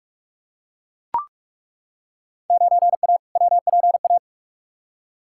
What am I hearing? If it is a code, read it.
Morse code, heard as a keyed tone, about 43 wpm, 700 Hz. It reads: 9A WPA
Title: Field Day Exchanges Encore - 45wpm